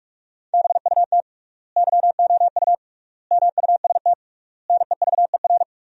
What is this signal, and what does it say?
Morse code, heard as a keyed tone, about 46 wpm, 700 Hz. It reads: BUT YOU MUST NEVER
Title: 3x Word-Spacing - Sentences from Top 300 Words - 45wpm